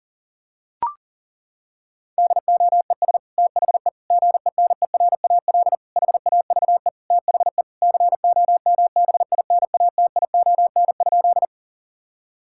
Morse code, heard as a keyed tone, about 40 wpm, 700 Hz. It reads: DOES THE GENERAL HAVE THE COMBINATION?
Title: Sentences from Top 1000 Words - Review Part 6 - 40wpm